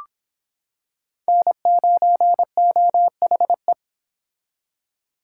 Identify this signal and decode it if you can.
Morse code, heard as a keyed tone, about 26 wpm, 700 Hz. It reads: N9OHE